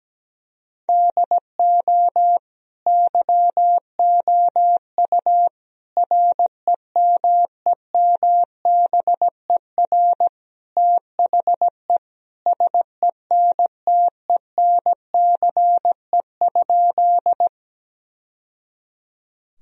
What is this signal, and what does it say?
Morse code, heard as a keyed tone, about 17 wpm, 700 Hz. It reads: DO YOU REMEMBER THE SENTENCE?
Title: Sentences from Top 1000 Words - Review Part 1 - 17wpm